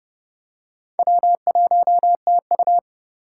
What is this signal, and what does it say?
Morse code, heard as a keyed tone, about 29 wpm, 700 Hz. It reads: W1TU